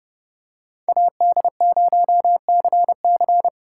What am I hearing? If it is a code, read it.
Morse code, heard as a keyed tone, about 28 wpm, 700 Hz. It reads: AD0CC